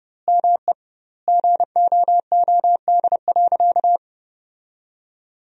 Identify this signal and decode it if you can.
Morse code, heard as a keyed tone, about 30 wpm, 700 Hz. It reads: ME GOOD.
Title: 2x Word-Spacing - Sentences from Top 200 Words - 30wpm